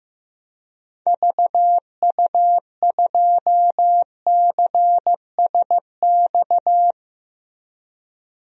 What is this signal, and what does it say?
Morse code, heard as a keyed tone, about 15 wpm, 700 Hz. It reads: VU2CSX
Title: DXCC-100 Call Signs - 6 Characters - 15wpm